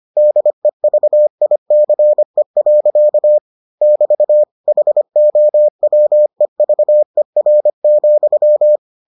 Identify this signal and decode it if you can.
Morse code, heard as a keyed tone, about 25 wpm, 600 Hz. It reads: DEVICE. <BT> HOWEVER,